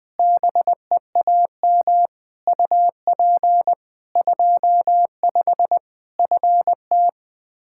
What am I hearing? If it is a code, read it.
Morse code, heard as a keyed tone, about 20 wpm, 700 Hz. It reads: BEAM UP 25 FT
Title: Practice QSO - Exchange 2 - Info - 20wpm